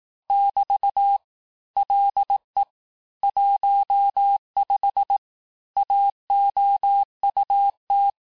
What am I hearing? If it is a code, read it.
Morse code, heard as a keyed tone, about 18 wpm, 800 Hz. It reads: <BT> LE 15 AOUT
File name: cours-du-lundi files / cours-1594--teste-clair-vitesse-17mpm-puis-22mpm